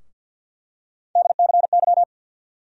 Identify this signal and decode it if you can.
Morse code, heard as a keyed tone, about 48 wpm, 700 Hz. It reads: DCY
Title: USA Call Sign Postfixes - 3 Letters 50wpm